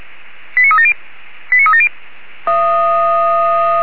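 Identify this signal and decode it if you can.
Five-tone selective call (ZVEI), heard as quick runs of five tones, tones of about 70 ms. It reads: 89370 89370
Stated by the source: ZVEI five-tone selective call received by SDR.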